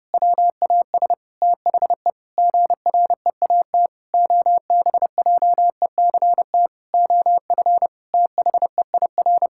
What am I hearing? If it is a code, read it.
Morse code, heard as a keyed tone, about 32 wpm, 700 Hz. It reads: WAS THE GREAT OBJECT OF THEIR
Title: Sentences from Top 1000 Words - Review Part 5 - 30wpm